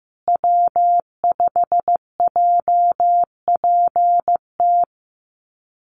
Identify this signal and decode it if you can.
Morse code, heard as a keyed tone, about 15 wpm, 700 Hz. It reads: W5JPT
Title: USA Call Signs Repeated 2x 15wpm